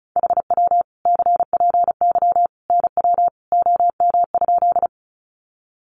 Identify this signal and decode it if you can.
Morse code, heard as a keyed tone, about 34 wpm, 700 Hz. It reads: HW CPY NW OM?
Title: Common QSO Phrases 35wpm